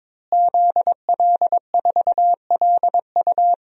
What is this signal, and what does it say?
Morse code, heard as a keyed tone, about 22 wpm, 700 Hz. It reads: ZL4LU
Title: DXCC-100 Call Signs - 5 Characters - Repeated 3x - 22wpm